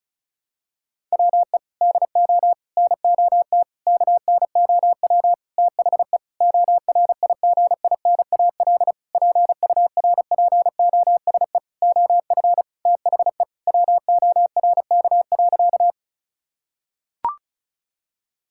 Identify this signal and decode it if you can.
Morse code, heard as a keyed tone, about 36 wpm, 700 Hz. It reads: WE DO NOT KNOW THE ORIGINAL PURPOSE OF THE WORK.
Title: Sentences from Top 1000 Words - Review Part 6 - 35wpm